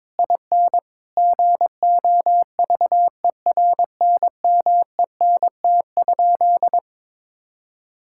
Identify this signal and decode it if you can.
Morse code, heard as a keyed tone, about 22 wpm, 700 Hz. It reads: IN GOVERNMENT?
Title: Sentences from Top 1000 Words - Review Part 6 - 22wpm